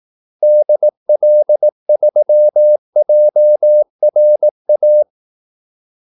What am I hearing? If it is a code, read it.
Morse code, heard as a keyed tone, about 18 wpm, 600 Hz. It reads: DL3JRA
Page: Kaldesignaler 18 wpm